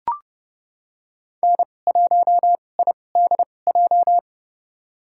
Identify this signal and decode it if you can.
Morse code, heard as a keyed tone, about 30 wpm, spaced as for 22 wpm, 700 Hz. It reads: N1IDJ
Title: USA Call Signs - 5 Characters - 2x Character Spacing - 30wpm